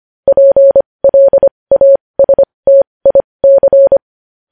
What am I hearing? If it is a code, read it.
Morse code, heard as a keyed tone, about 25 wpm, spaced as for 20 wpm, 550 Hz. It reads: PLASTIC
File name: podcast / News headlines-1x-25x20